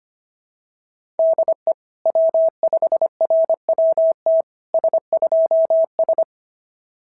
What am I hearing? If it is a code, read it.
Morse code, heard as a keyed tone, about 25 wpm, 690 Hz. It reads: DE W5RWT S2S
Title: SOTA QSO Practice 25wpm